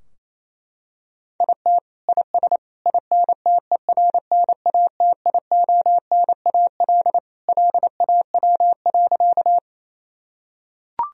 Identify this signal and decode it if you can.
Morse code, heard as a keyed tone, about 28 wpm, 700 Hz. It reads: IT IS INTERNATIONAL LAW.
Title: Sentences from Top 1000 Words - Review Part 4 - 28wpm